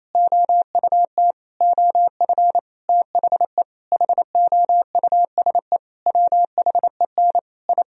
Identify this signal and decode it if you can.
Morse code, heard as a keyed tone, about 28 wpm, 700 Hz. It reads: OUT OF THE HOUSE WHEN I
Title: Sentences from Top 1000 Words - Review Part 4 - 28wpm